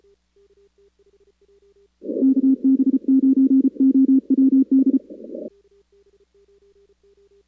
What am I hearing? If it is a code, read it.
Morse code, heard as a keyed tone, about 34 wpm, 270 Hz. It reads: KB9OWD
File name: contests / 2019-cw-wpx-kt5j-k5pi- -2330 0-4